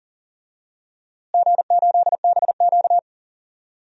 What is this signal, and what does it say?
Morse code, heard as a keyed tone, about 40 wpm, 700 Hz. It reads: G8BQ
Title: DXCC-100 Call Signs - 4 Characters - Repeated 3x - 40wpm